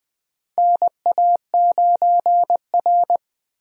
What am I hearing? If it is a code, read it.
Morse code, heard as a keyed tone, about 20 wpm, 700 Hz. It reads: NA9R
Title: USA Call Signs - 4 Characters - Repeated 2x - 20wpm